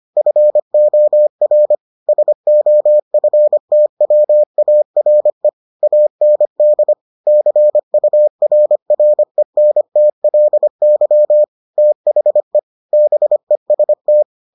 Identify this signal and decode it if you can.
Morse code, heard as a keyed tone, about 25 wpm, 600 Hz. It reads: FOR SOFTWARE AND CURRENTLY THE BEST